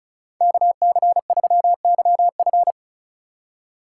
Morse code, heard as a keyed tone, about 34 wpm, 700 Hz. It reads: KC3YF
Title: USA Call Signs - 5 Characters - 4x Character Spacing - 35wpm